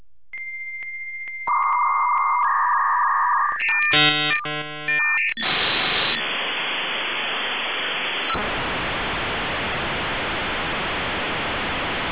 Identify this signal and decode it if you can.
Dial-up modem handshake.